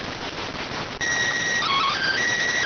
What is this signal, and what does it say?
SSTV (slow-scan television), PD 290.